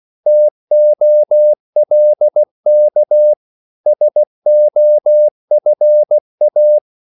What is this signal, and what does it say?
Morse code, heard as a keyed tone, about 16 wpm, 600 Hz. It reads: TOLK SOFA
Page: Korte danske ord 16wpm